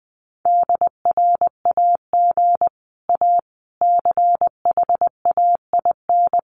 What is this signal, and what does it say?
Morse code, heard as a keyed tone, about 20 wpm, 700 Hz. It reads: DRAG A CHAIN
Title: Sentences from Top 1500 Words 20wpm